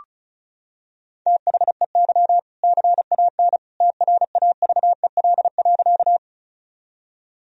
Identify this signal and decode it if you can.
Morse code, heard as a keyed tone, about 34 wpm, 700 Hz. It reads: THEY CAN TRAVEL.
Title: Sentences from Top 1000 Words - Review Part 2 - 35wpm